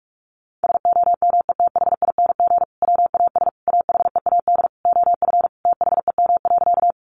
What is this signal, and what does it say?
Morse code, heard as a keyed tone, about 44 wpm, 700 Hz. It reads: SOMETHING WAS AHEAD OF THEM.
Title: Sentences from Top 1100 Words 45wpm